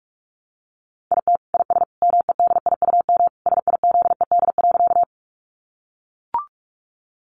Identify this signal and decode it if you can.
Morse code, heard as a keyed tone, about 44 wpm, 700 Hz. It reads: IT IS MEDIUM SIZED.
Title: Easy Phrases 45wpm